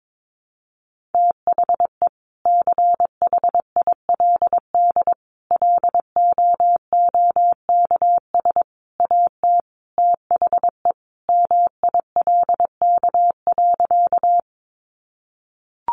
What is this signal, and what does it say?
Morse code, heard as a keyed tone, about 22 wpm, 700 Hz. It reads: THE CHILD LOOKS AT THE MILK.
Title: Additional Easy Phrases 22wpm